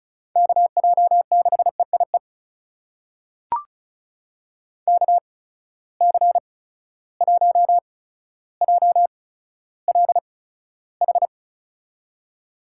Morse code, heard as a keyed tone, about 36 wpm, spaced as for 15 wpm, 700 Hz. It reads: KJ6EIE KC1JLH
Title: USA Call Signs - 6 Characters - 8x Character Spacing - 35wpm